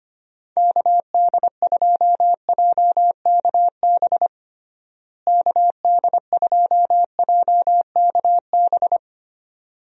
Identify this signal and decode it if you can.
Morse code, heard as a keyed tone, about 25 wpm, 700 Hz. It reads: KD2JKB KD2JKB
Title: USA Call Signs - 6 Characters - Repeated 2x - 25wpm